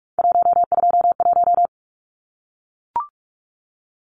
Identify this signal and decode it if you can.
Morse code, heard as a keyed tone, about 44 wpm, 700 Hz. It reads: 121
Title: Numbers - 3-Digit 45wpm